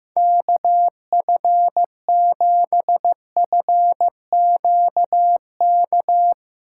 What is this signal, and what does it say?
Morse code, heard as a keyed tone, about 15 wpm, 700 Hz. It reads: KF7FQK
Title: USA Callsign DE Callsign - 6 Characters - 15wpm